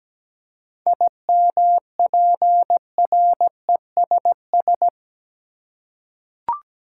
Morse code, heard as a keyed tone, about 17 wpm, 700 Hz. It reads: IMPRESS
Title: Seven Letter Words - 17wpm